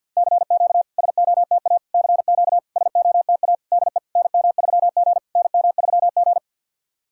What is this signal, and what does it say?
Morse code, heard as a keyed tone, about 53 wpm, 690 Hz. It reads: CQ SOTA CQ SOTA DE NM3G NM3G
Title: SOTA QSO Practice 50wpm